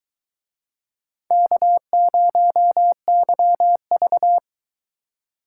Morse code, heard as a keyed tone, about 23 wpm, 700 Hz. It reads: K0YV